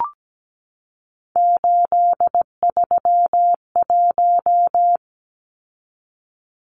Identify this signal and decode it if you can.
Morse code, heard as a keyed tone, about 17 wpm, 700 Hz. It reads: 831
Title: Numbers - 3-Digit 17wpm